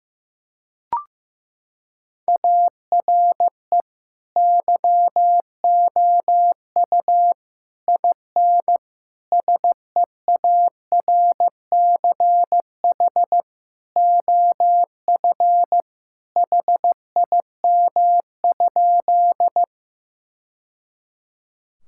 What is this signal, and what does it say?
Morse code, heard as a keyed tone, about 15 wpm, 700 Hz. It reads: ARE YOU IN SEARCH OF HIM?
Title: Sentences from Top 1000 Words - Review Part 5 - 15wpm